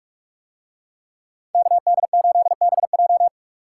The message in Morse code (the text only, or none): KD8BJ